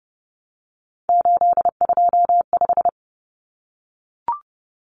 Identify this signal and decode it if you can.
Morse code, heard as a keyed tone, about 30 wpm, 700 Hz. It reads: 825
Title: Common Cut Numbers - 3-Digit 30wpm